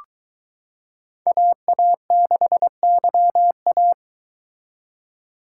Morse code, heard as a keyed tone, about 22 wpm, 700 Hz. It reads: AA6YA